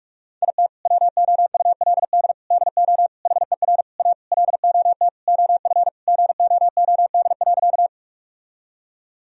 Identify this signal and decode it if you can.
Morse code, heard as a keyed tone, about 46 wpm, 700 Hz. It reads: IT WOULD DO HER A LOT OF GOOD.
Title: Sentences from Top 1000 Words - Review Part 3 - 45wpm